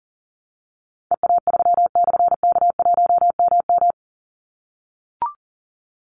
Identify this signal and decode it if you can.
Morse code, heard as a keyed tone, about 40 wpm, 700 Hz. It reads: EA3/K1MM
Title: Call Signs - 40wpm